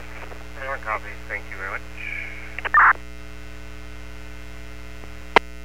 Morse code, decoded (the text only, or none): none